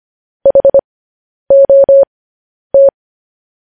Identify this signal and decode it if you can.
Morse code, heard as a keyed tone, about 25 wpm, spaced as for 10 wpm, 550 Hz. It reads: HOT
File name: podcast / News headlines-1x-25x10